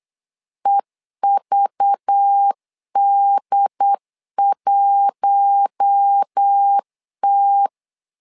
Morse code, heard as a keyed tone, about 8 wpm, 800 Hz. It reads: EVD1T
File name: LEZIONI / 5 LBVC / 5 LBVC 8wpm 2935